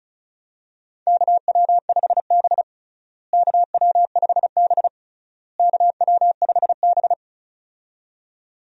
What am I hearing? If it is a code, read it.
Morse code, heard as a keyed tone, about 34 wpm, 700 Hz. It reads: KW5B KW5B KW5B
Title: USA Call Signs - 4 Characters - Repeated 3x - 35wpm